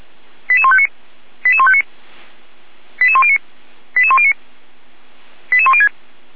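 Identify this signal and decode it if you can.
Five-tone selective call (ZVEI), heard as quick runs of five tones, tones of about 70 ms. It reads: 88159 88159 88109 88109 88106
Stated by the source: Received by SDR on a Feuerwehr Orlamünde server.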